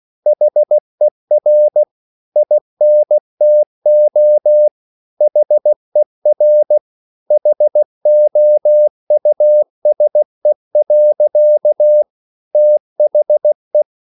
Morse code, heard as a keyed tone, about 16 wpm, 600 Hz. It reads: HER INTO HER HOUSE. THE